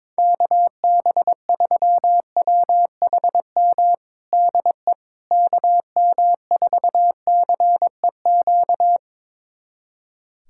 Morse code, heard as a keyed tone, about 21 wpm, 700 Hz. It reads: KB3WHM DE KM4CEQ
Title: USA Callsign DE Callsign - 6 Characters - 22wpm